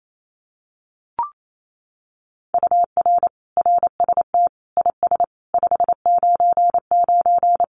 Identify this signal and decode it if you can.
Morse code, heard as a keyed tone, about 28 wpm, 700 Hz. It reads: UR RST IS 599
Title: Common QSO Phrases Encore 28wpm